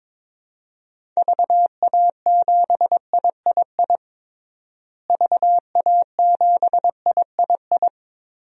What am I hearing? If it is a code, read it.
Morse code, heard as a keyed tone, about 22 wpm, 700 Hz. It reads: VA7III VA7III
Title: DXCC-100 Call Signs - 6 Characters - Repeated 3x - 22wpm